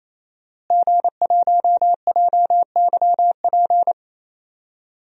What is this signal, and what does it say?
Morse code, heard as a keyed tone, about 27 wpm, 700 Hz. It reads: G1JYP
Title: DXCC-100 Call Signs - 5 Characters - Repeated 3x - 28wpm